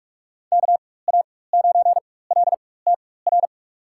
Morse code, heard as a keyed tone, about 46 wpm, spaced as for 21 wpm, 700 Hz. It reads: KA9LTR
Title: USA Call Signs - 6 Characters - 4x Character Spacing - 45wpm